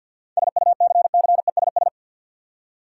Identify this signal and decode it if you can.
Morse code, heard as a keyed tone, about 51 wpm, 700 Hz. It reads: SUCCESS
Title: Seven Letter Words - 50wpm